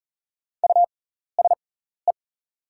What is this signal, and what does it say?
Morse code, heard as a keyed tone, about 40 wpm, spaced as for 14 wpm, 700 Hz. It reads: USE
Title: 6x Character Spacing - Top 100 Words - 40wpm